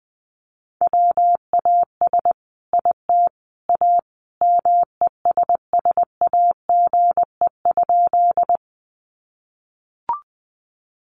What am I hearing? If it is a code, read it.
Morse code, heard as a keyed tone, about 20 wpm, 700 Hz. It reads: WAS IT A MESSAGE?
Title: Sentences from Top 1100 Words 20wpm